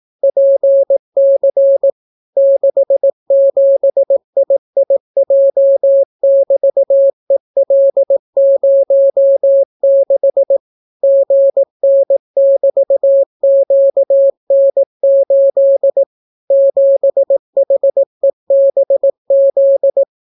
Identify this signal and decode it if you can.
Morse code, heard as a keyed tone, about 18 wpm, 550 Hz. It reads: PC 67IIJ<BT>EL06 GN<BT>QN8 7HEBZ